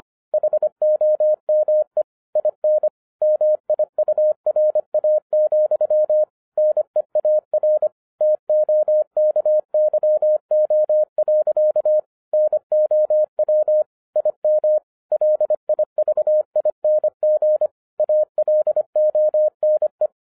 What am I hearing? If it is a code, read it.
Morse code, heard as a keyed tone, about 25 wpm, 610 Hz. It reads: HOME IN MIURA, NEAR TOKYO. NOW IM LIVING ALONE
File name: 2011-10-mp3 / n04-JUST MOVED 25wpm